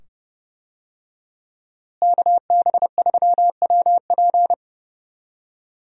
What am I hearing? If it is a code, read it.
Morse code, heard as a keyed tone, about 30 wpm, 700 Hz. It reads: KB3WP
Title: USA Call Signs - 5 Characters - Repeated 2x - 30wpm